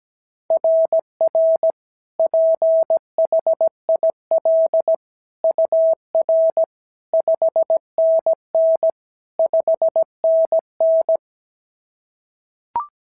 Morse code, heard as a keyed tone, about 16 wpm, 650 Hz. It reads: RR PHIL UR 5NN 5NN
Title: SOTA QSO Practice 17wpm